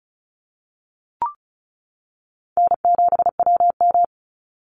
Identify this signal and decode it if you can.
Morse code, heard as a keyed tone, about 34 wpm, 700 Hz. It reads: N7WM
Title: NA SOTA Chaser Callsigns 35wpm